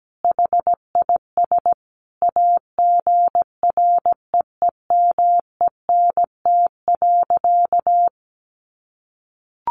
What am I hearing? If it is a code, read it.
Morse code, heard as a keyed tone, about 17 wpm, 700 Hz. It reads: HIS AGREEMENT.
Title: Sentences from Top 1500 Words 17wpm